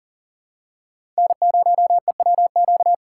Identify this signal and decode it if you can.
Morse code, heard as a keyed tone, about 40 wpm, 700 Hz. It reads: N0EWQ
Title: USA Call Signs - 5 Characters - 2x Character Spacing - 40wpm